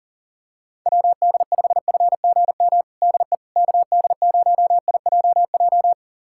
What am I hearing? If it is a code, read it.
Morse code, heard as a keyed tone, about 40 wpm, 700 Hz. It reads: WD5FGM DE KD0IJJ
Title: USA Callsign DE Callsign - 6 Characters - 40wpm